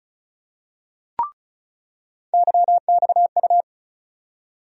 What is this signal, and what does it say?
Morse code, heard as a keyed tone, about 36 wpm, 700 Hz. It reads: YXU
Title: USA Call Sign Postfixes - 3 Letters 35wpm